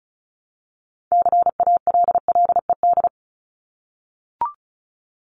Morse code, heard as a keyed tone, about 34 wpm, 700 Hz. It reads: CALLED
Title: ED Suffix - Top 100 Words with Base Words 35wpm